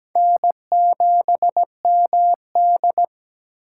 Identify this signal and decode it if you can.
Morse code, heard as a keyed tone, about 17 wpm, 700 Hz. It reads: N7MD
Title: USA Call Signs - 6x Character Spacing - 17wpm